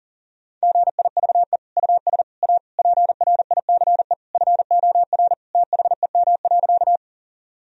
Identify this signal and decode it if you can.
Morse code, heard as a keyed tone, about 40 wpm, 700 Hz. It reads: GIVE US A PRICE FOR THEM.
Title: Sentences from Top 1000 Words - Review Part 4 - 40wpm